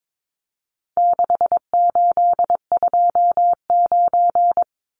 Morse code, heard as a keyed tone, about 22 wpm, 700 Hz. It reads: 6829
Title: Numbers - 4-Digit 22wpm